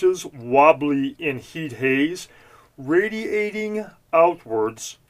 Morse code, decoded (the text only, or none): none